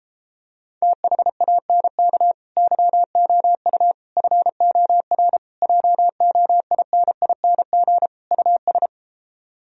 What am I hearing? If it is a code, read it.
Morse code, heard as a keyed tone, about 33 wpm, 700 Hz. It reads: THANK YOU FOR JOINING US